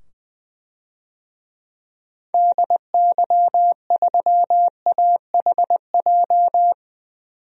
Morse code, heard as a keyed tone, about 21 wpm, 700 Hz. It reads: DY3AHJ
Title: DXCC-100 Call Signs - 6 Characters - 20wpm